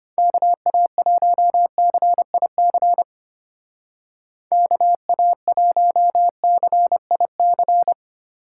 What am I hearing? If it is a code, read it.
Morse code, heard as a keyed tone, about 32 wpm, 700 Hz. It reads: KA1CIC KA1CIC